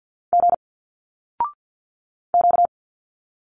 Morse code, heard as a keyed tone, about 50 wpm, 700 Hz. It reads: G Q
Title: Single Letters 50wpm